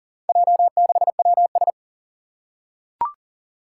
Morse code, heard as a keyed tone, about 40 wpm, 700 Hz. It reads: J6WS
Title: Four Challenging Letter-Number Combinations 40wpm